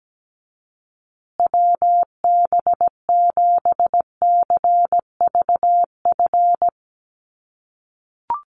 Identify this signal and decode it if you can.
Morse code, heard as a keyed tone, about 17 wpm, 700 Hz. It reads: WB7CVF
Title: USA Call Signs Repeated 2x 17wpm